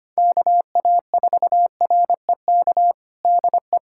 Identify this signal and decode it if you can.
Morse code, heard as a keyed tone, about 25 wpm, 700 Hz. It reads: KA4REK DE
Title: USA Callsign DE Callsign 25wpm